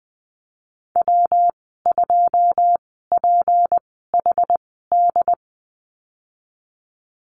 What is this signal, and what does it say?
Morse code, heard as a keyed tone, about 20 wpm, spaced as for 15 wpm, 700 Hz. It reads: W2PHD
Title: 154 Lesson 25g - Letter p - Call Signs 20wpm